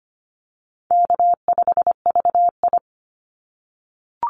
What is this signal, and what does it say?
Morse code, heard as a keyed tone, about 25 wpm, 700 Hz. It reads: K5VI
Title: Easy Call Signs - 25wpm